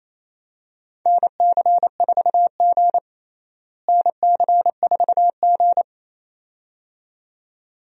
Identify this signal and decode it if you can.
Morse code, heard as a keyed tone, about 28 wpm, 700 Hz. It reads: NC4G NC4G
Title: USA Call Signs - 4 Characters - Repeated 2x - 28wpm